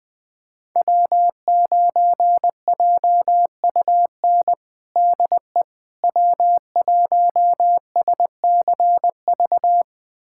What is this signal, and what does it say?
Morse code, heard as a keyed tone, about 20 wpm, 700 Hz. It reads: W9JUN DE W1SCV
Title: USA Callsign DE Callsign - 5 Characters - 20wpm